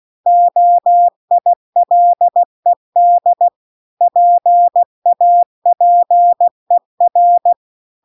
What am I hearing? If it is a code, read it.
Morse code, heard as a keyed tone, about 16 wpm, 700 Hz. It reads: OILED PAPER
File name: CW-The-Big-Twoheaded-River-2-16WPM